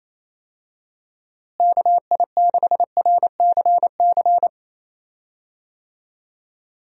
Morse code, heard as a keyed tone, about 28 wpm, 700 Hz. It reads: KI6RCC
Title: USA Call Signs - 6 Characters - Repeated 2x - 28wpm